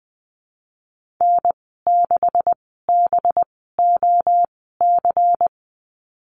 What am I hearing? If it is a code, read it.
Morse code, heard as a keyed tone, about 20 wpm, spaced as for 15 wpm, 700 Hz. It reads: N6BOC
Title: 186 Lesson 30g - Letter b - Call Signs 20wpm